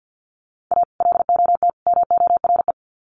Morse code, heard as a keyed tone, about 50 wpm, 700 Hz. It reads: A LOT MORE
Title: Sets of 3 Words 50wpm